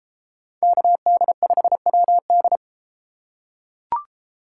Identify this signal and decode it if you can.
Morse code, heard as a keyed tone, about 31 wpm, 700 Hz. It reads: KD5WD